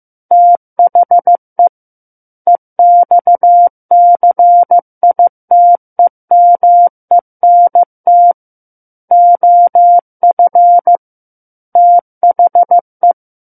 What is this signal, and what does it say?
Morse code, heard as a keyed tone, about 15 wpm, 700 Hz. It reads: THE EXCITEMENT OF THE
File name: Book One / 06-Chapter 6 - 15 WPM